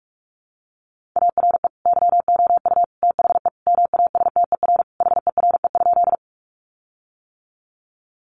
Morse code, heard as a keyed tone, about 44 wpm, 700 Hz. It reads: ARE YOU THE MASTER HERE?
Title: Sentences from Top 1300 Words 45wpm